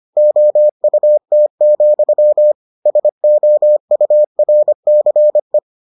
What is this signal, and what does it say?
Morse code, heard as a keyed tone, about 25 wpm, 600 Hz. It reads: OUT, SOURCE